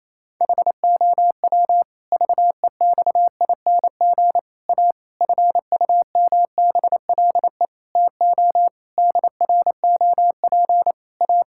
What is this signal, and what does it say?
Morse code, heard as a keyed tone, about 28 wpm, 700 Hz. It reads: HOW VEXING A FUMBLE TO DROP A
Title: Pangram Challenge v2 - 28wpm